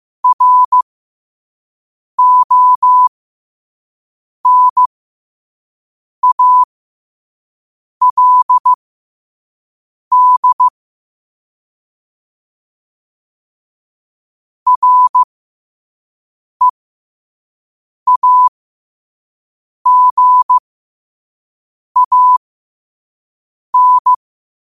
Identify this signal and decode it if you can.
Morse code, heard as a keyed tone, about 15 wpm, spaced as for 5 wpm, 1000 Hz. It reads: RONALD REAGAN
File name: qotdfiles / 5wpm